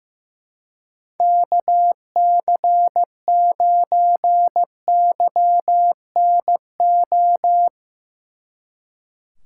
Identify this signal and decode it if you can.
Morse code, heard as a keyed tone, about 17 wpm, 700 Hz. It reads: KC9YNO
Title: Speed Racing - USA 2x3 Call Signs 15wpm